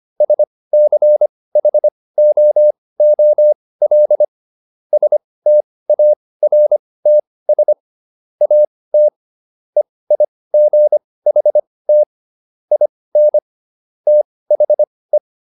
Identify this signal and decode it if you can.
Morse code, heard as a keyed tone, about 25 wpm, spaced as for 18 wpm, 600 Hz. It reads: SCHOOL STARTS AT EIGHT IN THE